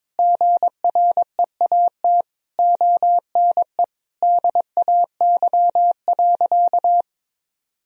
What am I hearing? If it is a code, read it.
Morse code, heard as a keyed tone, about 22 wpm, 700 Hz. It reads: GREAT ONE DAY.
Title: Sentences from Top 1000 Words - Review Part 1 - 22wpm